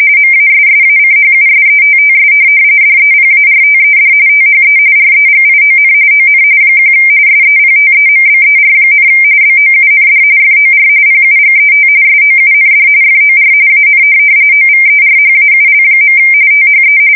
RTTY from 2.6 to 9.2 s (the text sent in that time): DEPARTED THE REGION ON 1 SEPTEMBER,